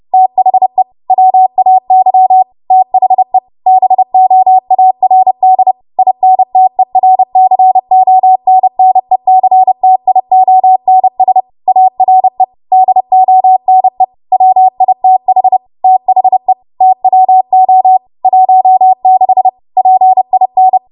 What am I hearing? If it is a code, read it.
Morse code, heard as a keyed tone, about 30 wpm, 750 Hz. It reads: THE WAY THE BOARD INTERCONNECTIONS ARE DONE WITH THE TWO 16 PIN
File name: Archive / 30 WPM / 30WPM